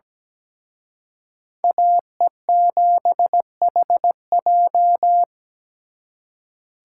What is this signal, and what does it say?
Morse code, heard as a keyed tone, about 20 wpm, 700 Hz. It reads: AE7HJ
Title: Speed Racing - USA 2x2 Call Signs 17wpm